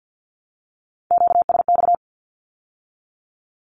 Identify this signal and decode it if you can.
Morse code, heard as a keyed tone, about 50 wpm, 700 Hz. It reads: QSX
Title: Q-Codes with Spoken Meaning - 50wpm